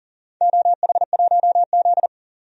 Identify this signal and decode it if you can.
Morse code, heard as a keyed tone, about 42 wpm, 690 Hz. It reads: OH1Z
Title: DXCC-100 Call Signs - 4 Characters - 40wpm